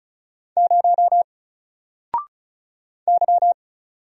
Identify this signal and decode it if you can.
Morse code, heard as a keyed tone, about 34 wpm, 700 Hz. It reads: 0 Y
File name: Single-Letter-Number-35wpm